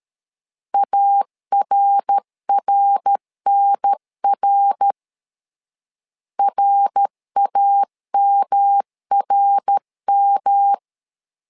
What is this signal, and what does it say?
Morse code, heard as a keyed tone, about 13 wpm, 800 Hz. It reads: ARRNR RAMRM
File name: LEZIONI / 3 MWRO / 3 MWR 12wpm 1711